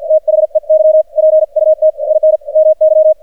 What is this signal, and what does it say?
Morse code, heard as a keyed tone, about 48 wpm, 620 Hz. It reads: THEY PUT IN MY
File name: archived / CRSnet